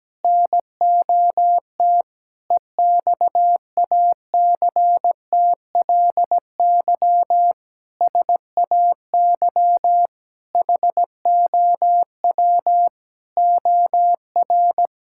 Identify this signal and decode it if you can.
Morse code, heard as a keyed tone, about 17 wpm, 700 Hz. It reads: NOT EXACTLY SAY HOW OR
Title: Sentences from Top 1000 Words - Review Part 6 - 17wpm